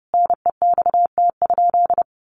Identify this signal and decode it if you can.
Morse code, heard as a keyed tone, about 30 wpm, 700 Hz. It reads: NEXT?
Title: Sentences from Top 1200 Words 30wpm